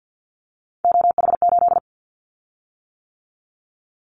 Morse code, heard as a keyed tone, about 50 wpm, 700 Hz. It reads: OH8
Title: Callsign Prefixes for Top 50 DXCC - 50wpm